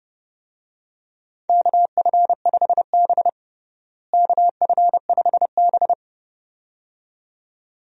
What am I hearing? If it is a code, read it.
Morse code, heard as a keyed tone, about 32 wpm, 700 Hz. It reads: KF5B KF5B
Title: USA Call Signs - 4 Characters - Repeated 2x - 30wpm